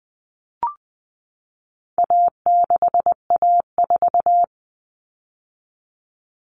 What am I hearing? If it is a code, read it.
Morse code, heard as a keyed tone, about 20 wpm, 700 Hz. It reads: A6A4
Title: All Cut Numbers - 4-Digit 20wpm